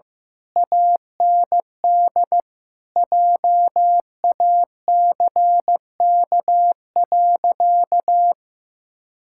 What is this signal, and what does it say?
Morse code, heard as a keyed tone, about 16 wpm, 700 Hz. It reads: AND JACK.
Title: Pangram Challenge v2 - 15wpm